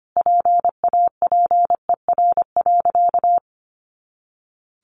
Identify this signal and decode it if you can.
Morse code, heard as a keyed tone, about 25 wpm, 700 Hz. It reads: PAPER.
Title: Sentences from Top 300 Words 25wpm